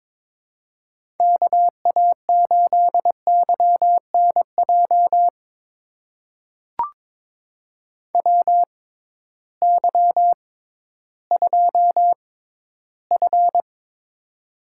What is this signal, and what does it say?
Morse code, heard as a keyed tone, about 22 wpm, spaced as for 13 wpm, 700 Hz. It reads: KA8YNJ WY2F
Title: USA Call Signs - 6x Character Spacing - 22wpm